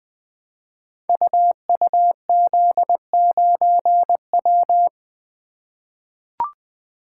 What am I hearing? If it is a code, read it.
Morse code, heard as a keyed tone, about 20 wpm, 700 Hz. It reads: UUZ9W
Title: Code Groups - 5 Letter:Numbers - 20wpm